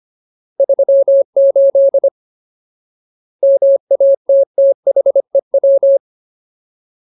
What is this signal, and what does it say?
Morse code, heard as a keyed tone, about 25 wpm, 550 Hz. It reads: 38 MATTHEW
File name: Ultimate Word List / 25WPM Ultimate